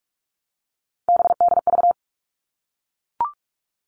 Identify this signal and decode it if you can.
Morse code, heard as a keyed tone, about 44 wpm, 700 Hz. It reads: BDV
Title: All Cut Numbers - 3-Digit 45wpm